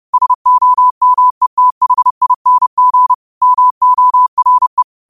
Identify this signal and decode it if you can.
Morse code, heard as a keyed tone, about 30 wpm, 1000 Hz. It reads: SOMETHING MORE
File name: qotdfiles / 30wpm